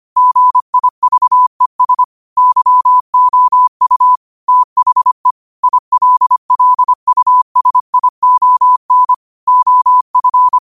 Morse code, heard as a keyed tone, about 25 wpm, 1000 Hz. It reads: GIVES YOU THE ILLUSION OF
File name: qotdfiles / 25wpm